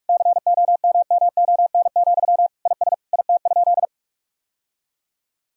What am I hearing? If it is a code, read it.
Morse code, heard as a keyed tone, about 46 wpm, 700 Hz. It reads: COMMON, IS IT?
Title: Sentences from Top 1000 Words - Review Part 3 - 45wpm